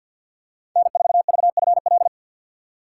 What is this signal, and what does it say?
Morse code, heard as a keyed tone, about 52 wpm, 700 Hz. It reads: N4VFL
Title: USA Call Signs - 2x Character Spacing - 50wpm